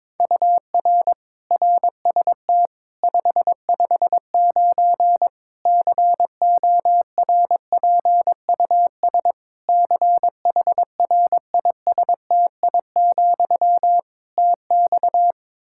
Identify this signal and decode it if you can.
Morse code, heard as a keyed tone, about 21 wpm, 700 Hz. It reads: UR RST 559 CORPUS CHRISTI, TX
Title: Practice QSO - Exchange 1 - Beginning - 22wpm